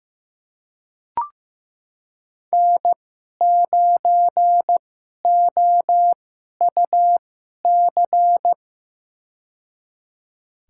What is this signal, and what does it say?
Morse code, heard as a keyed tone, about 15 wpm, spaced as for 11 wpm, 700 Hz. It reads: N9OUC
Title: USA Call Signs - 5 Characters - 2x Character Spacing - 15wpm